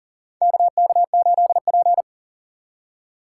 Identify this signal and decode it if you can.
Morse code, heard as a keyed tone, about 37 wpm, 700 Hz. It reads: KK8P
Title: USA Call Signs - 6x Character Spacing - 40wpm